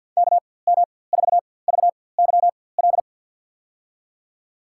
Morse code, heard as a keyed tone, about 52 wpm, spaced as for 24 wpm, 700 Hz. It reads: KM4VYL
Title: USA Call Signs - 6 Characters - 4x Character Spacing - 50wpm